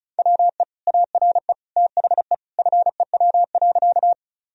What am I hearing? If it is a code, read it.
Morse code, heard as a keyed tone, about 34 wpm, 700 Hz. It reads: WE ARE THE FEW.
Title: Sentences from Top 1000 Words - Review Part 1 - 35wpm